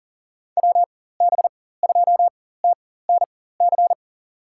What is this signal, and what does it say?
Morse code, heard as a keyed tone, about 40 wpm, spaced as for 19 wpm, 700 Hz. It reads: WB2TNC
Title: USA Call Signs - 6 Characters - 4x Character Spacing - 40wpm